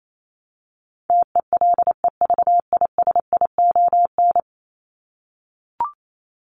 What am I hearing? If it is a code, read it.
Morse code, heard as a keyed tone, about 28 wpm, 700 Hz. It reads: TELEVISION
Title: Top 501-750 Words 28wpm